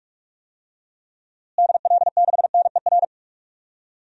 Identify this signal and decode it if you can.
Morse code, heard as a keyed tone, about 46 wpm, 700 Hz. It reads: DL6NER
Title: DXCC-100 Call Signs - 6 Characters - Repeated 3x - 45wpm